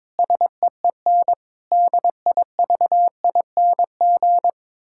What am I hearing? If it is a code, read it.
Morse code, heard as a keyed tone, about 22 wpm, 700 Hz. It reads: SEEN DIVING
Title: Pangram Challenge v2 - 22wpm